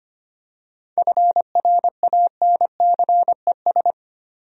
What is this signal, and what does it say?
Morse code, heard as a keyed tone, about 25 wpm, 700 Hz. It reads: FRANCES
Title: USA Common Names - 25wpm